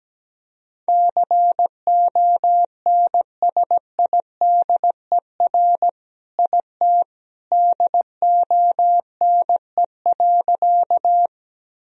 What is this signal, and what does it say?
Morse code, heard as a keyed tone, about 17 wpm, 700 Hz. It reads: CONSIDER IT DONE.
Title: Sentences from Top 1000 Words - Review Part 4 - 17wpm